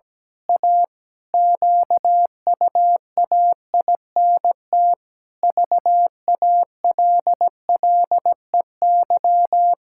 Morse code, heard as a keyed tone, about 17 wpm, 700 Hz. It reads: A QUAINT VALLEY
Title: Pangram Challenge v2 - 17wpm